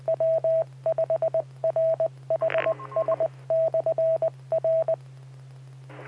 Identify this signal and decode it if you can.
Morse code, heard as a keyed tone, about 20 wpm, 650 Hz, with radio noise. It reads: W5RHS/R
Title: Saline, AK SKYWARN Repeater System - W5RHS